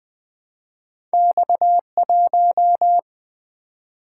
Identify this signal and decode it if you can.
Morse code, heard as a keyed tone, about 20 wpm, 700 Hz. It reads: X1
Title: Single-Letter Single-Number 20wpm